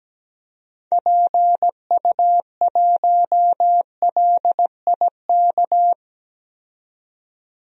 Morse code, heard as a keyed tone, about 17 wpm, 700 Hz. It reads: PU1LIK
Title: DXCC-100 Call Signs - 6 Characters - 17wpm